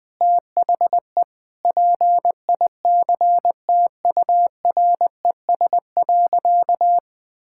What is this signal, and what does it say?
Morse code, heard as a keyed tone, about 20 wpm, 700 Hz. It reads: THE PICTURES.
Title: Sentences with 7-Word Phrases - 20wpm